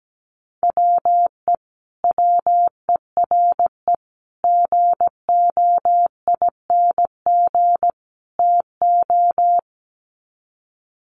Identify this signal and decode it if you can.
Morse code, heard as a keyed tone, about 17 wpm, 700 Hz. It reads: WE WERE GOING TO
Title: Sets of 4 Words 17wpm